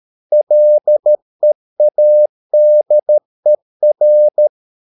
Morse code, heard as a keyed tone, about 13 wpm, 600 Hz. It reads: LEADER